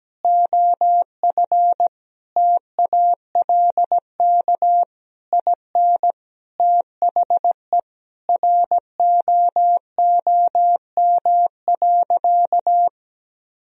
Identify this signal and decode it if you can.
Morse code, heard as a keyed tone, about 17 wpm, 700 Hz. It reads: OF TALK IN THE ROOM.
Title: Sentences from Top 1000 Words - Review Part 4 - 17wpm